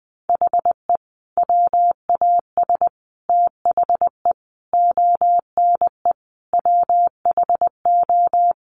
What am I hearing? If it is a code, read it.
Morse code, heard as a keyed tone, about 20 wpm, 700 Hz. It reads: HE WAS THE ONE WHO
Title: Sets of 5 Words Encore2 20wpm